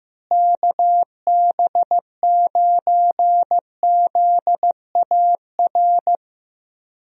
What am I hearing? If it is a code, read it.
Morse code, heard as a keyed tone, about 15 wpm, 700 Hz. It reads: KB9ZAR
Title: USA Call Signs - 6 Characters - Repeated 3x - 15wpm